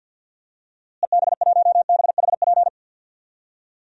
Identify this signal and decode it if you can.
Morse code, heard as a keyed tone, about 47 wpm, 700 Hz. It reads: EB1BHP
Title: DXCC-100 Call Signs - 6 Characters - Repeated 2x - 50wpm